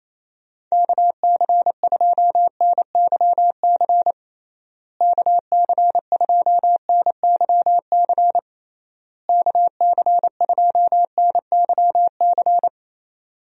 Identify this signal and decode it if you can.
Morse code, heard as a keyed tone, about 28 wpm, 700 Hz. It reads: KC2NYC KC2NYC KC2NYC
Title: USA Call Signs - 6 Characters - Repeated 3x - 28wpm